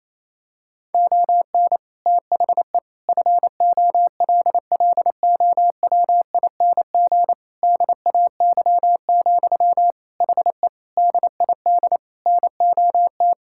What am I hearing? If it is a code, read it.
Morse code, heard as a keyed tone, about 28 wpm, 700 Hz. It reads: ON THE FOLLOWING DAY, HE DID NOT
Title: Sentences from Top 1000 Words - Review Part 6 - 28wpm